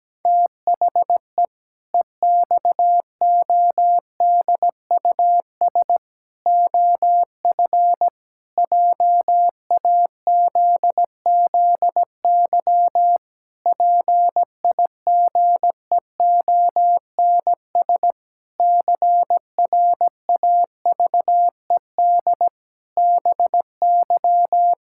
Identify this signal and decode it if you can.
Morse code, heard as a keyed tone, about 17 wpm, 700 Hz. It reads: THE EXODUS OF JAZZY PIGEONS CRAVED BY
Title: Pangram Challenge v2 - 17wpm